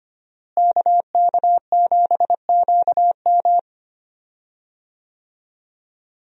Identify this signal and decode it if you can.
Morse code, heard as a keyed tone, about 25 wpm, 700 Hz. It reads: KK7QM
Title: USA Call Signs - 5 Characters - Repeated 3x - 25wpm